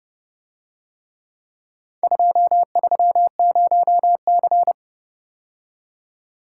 Morse code, heard as a keyed tone, about 29 wpm, 700 Hz. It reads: 230C